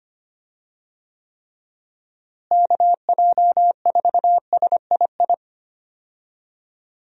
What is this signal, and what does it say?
Morse code, heard as a keyed tone, about 25 wpm, 700 Hz. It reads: KJ4SII
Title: USA Call Signs - 6 Characters - Repeated 2x - 25wpm